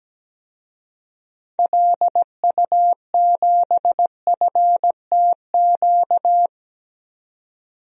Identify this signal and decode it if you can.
Morse code, heard as a keyed tone, about 17 wpm, 700 Hz. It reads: LU7FTQ
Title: DXCC-100 Call Signs - 6 Characters - 17wpm